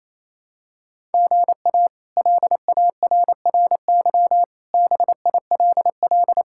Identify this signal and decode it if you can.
Morse code, heard as a keyed tone, about 27 wpm, 700 Hz. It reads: GA LARRY BILL
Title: K1USN SST QSO Practice - 28wpm